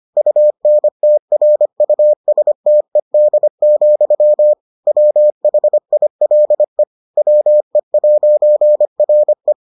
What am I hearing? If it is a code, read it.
Morse code, heard as a keyed tone, about 25 wpm, 600 Hz. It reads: UNTRUSTED, WHILE WE'RE